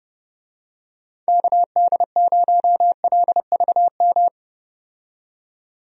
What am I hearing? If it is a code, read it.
Morse code, heard as a keyed tone, about 30 wpm, 700 Hz. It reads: KD0LVM
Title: USA Call Signs - 6 Characters - 30wpm